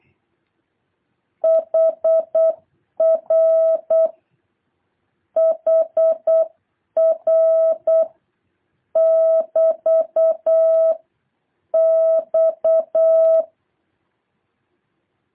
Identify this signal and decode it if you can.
Morse code, heard as a keyed tone, about 8 wpm, 650 Hz. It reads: HR HR <BT> X